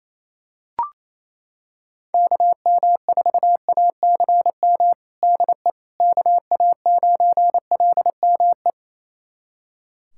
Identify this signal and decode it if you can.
Morse code, heard as a keyed tone, about 28 wpm, 700 Hz. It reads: KM4ACM DE KA9LME
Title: USA Callsign DE Callsign 28wpm